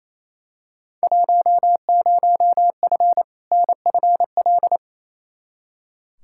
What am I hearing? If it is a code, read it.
Morse code, heard as a keyed tone, about 28 wpm, 700 Hz. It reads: 10F NFL
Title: Field Day Exchanges Encore - 28wpm